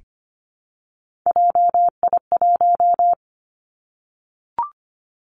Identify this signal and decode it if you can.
Morse code, heard as a keyed tone, about 25 wpm, 700 Hz. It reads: JI1
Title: Three Challenging Letter-Number Combinations 25wpm